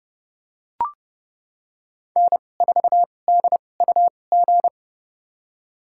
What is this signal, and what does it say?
Morse code, heard as a keyed tone, about 30 wpm, spaced as for 22 wpm, 700 Hz. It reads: N4DUG
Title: USA Call Signs - 5 Characters - 2x Character Spacing - 30wpm